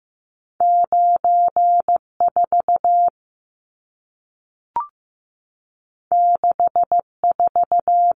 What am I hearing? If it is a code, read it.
Morse code, heard as a keyed tone, about 15 wpm, 700 Hz. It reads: 94 64
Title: Numbers - 2-Digit 15wpm